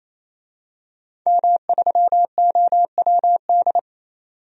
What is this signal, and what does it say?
Morse code, heard as a keyed tone, about 28 wpm, 700 Hz. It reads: M3OWD
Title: DXCC-100 Call Signs - 5 Characters - 28wpm